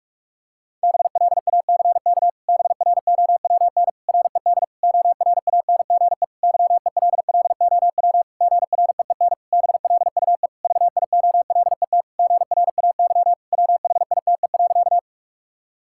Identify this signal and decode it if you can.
Morse code, heard as a keyed tone, about 46 wpm, 700 Hz. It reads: BLACK BROWN RED ORANGE YELLOW GREEN BLUE VIOLET GRAY WHITE.
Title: Pangram Challenge v2 - 45wpm